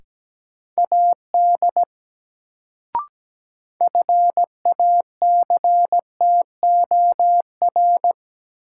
Morse code, heard as a keyed tone, about 17 wpm, 700 Hz. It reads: AD FACTOR
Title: Sentences from Top 1300 Words - New Word Drill - 17wpm